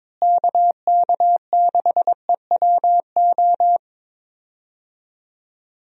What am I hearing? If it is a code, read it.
Morse code, heard as a keyed tone, about 22 wpm, 700 Hz. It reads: KK6EWO
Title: USA Call Signs - 6 Characters - Repeated 2x - 22wpm